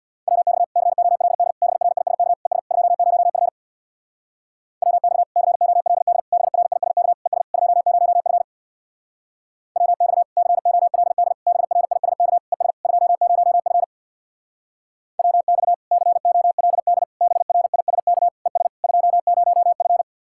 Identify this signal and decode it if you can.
Morse code, heard as a keyed tone, about 64 wpm, 700 Hz. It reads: WX COLD BRISK ES 20F WX COLD BRISK ES 20F WX COLD BRISK ES 20F WX COLD BRISK ES 20F